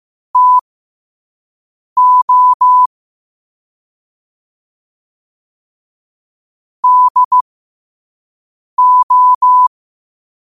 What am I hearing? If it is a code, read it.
Morse code, heard as a keyed tone, about 15 wpm, spaced as for 5 wpm, 1000 Hz. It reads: TO DO
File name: qotdfiles / 5wpm